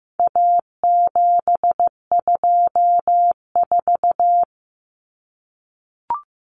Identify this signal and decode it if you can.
Morse code, heard as a keyed tone, about 15 wpm, 700 Hz. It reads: A724
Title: Common Cut Numbers - 4-Digit 15wpm